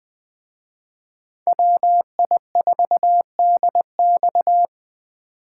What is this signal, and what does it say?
Morse code, heard as a keyed tone, about 24 wpm, 700 Hz. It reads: WI4DX
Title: Speed Racing - USA Call Signs 20wpm